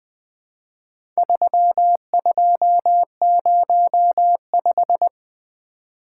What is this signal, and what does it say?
Morse code, heard as a keyed tone, about 20 wpm, 700 Hz. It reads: 3205
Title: Four Letter-Number Combinations - 20wpm